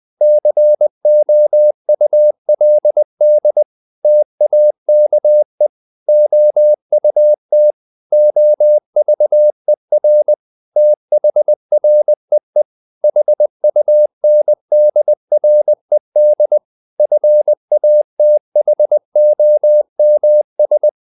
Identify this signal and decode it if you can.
Morse code, heard as a keyed tone, about 20 wpm, 600 Hz. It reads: COULD TAKE OUT OVER THREE HUNDRED FATHOMS
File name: the old man and the sea / the old man and the sea part 1 20wpm